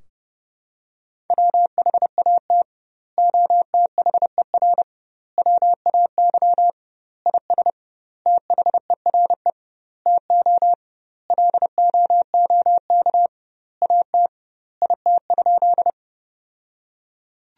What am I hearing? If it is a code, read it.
Morse code, heard as a keyed tone, about 30 wpm, 700 Hz. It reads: WHAT OTHER WAY IS THERE TO LOOK AT IT?
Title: 2x Word-Spacing - Sentences from Top 100 Words - 30wpm